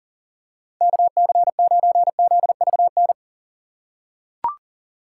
Morse code, heard as a keyed tone, about 40 wpm, 700 Hz. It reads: KC9ZVN
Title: USA Call Signs - 6 Characters - 2x Character Spacing - 40wpm